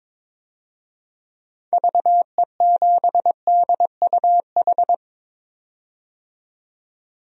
Morse code, heard as a keyed tone, about 22 wpm, 700 Hz. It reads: VE7DUH
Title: DXCC-100 Call Signs - 6 Characters - 22wpm